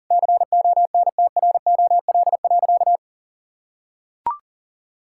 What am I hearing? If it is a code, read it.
Morse code, heard as a keyed tone, about 40 wpm, 700 Hz. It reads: CONTROL.
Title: Sentences from Top 1000 Words - Review Part 5 - 40wpm